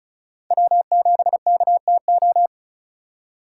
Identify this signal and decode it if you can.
Morse code, heard as a keyed tone, about 34 wpm, 700 Hz. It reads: W7KTO
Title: USA Call Signs - 5 Characters - 3x Character Spacing - 35wpm